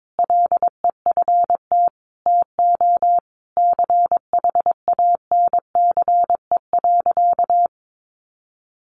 Morse code, heard as a keyed tone, about 22 wpm, 700 Hz. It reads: LEFT TO CHANCE.
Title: Sentences from Top 1100 Words 22wpm